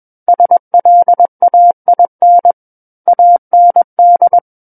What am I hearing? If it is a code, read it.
Morse code, heard as a keyed tone, about 21 wpm, 700 Hz. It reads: SLAIN AND
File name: Book One / 12-Chapter 12 - 21 WPM